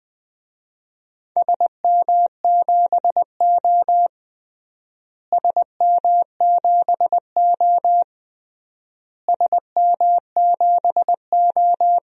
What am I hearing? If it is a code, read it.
Morse code, heard as a keyed tone, about 20 wpm, 700 Hz. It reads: SM7O SM7O SM7O
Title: DXCC-100 Call Signs - 4 Characters - Repeated 3x - 20wpm